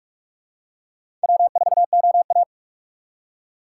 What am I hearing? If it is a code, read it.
Morse code, heard as a keyed tone, about 45 wpm, 700 Hz. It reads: W4OA
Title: USA Call Signs - 4 Characters - 2x Character Spacing - 45wpm